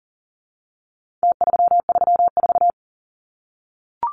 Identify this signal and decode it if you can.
Morse code, heard as a keyed tone, about 40 wpm, 700 Hz. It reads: T334
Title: Numbers - Contest Style - 4-Digit 40wpm